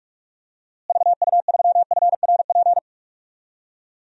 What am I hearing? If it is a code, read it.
Morse code, heard as a keyed tone, about 41 wpm, 700 Hz. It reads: VU3FRP
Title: DXCC-100 Call Signs - 6 Characters - Repeated 3x - 45wpm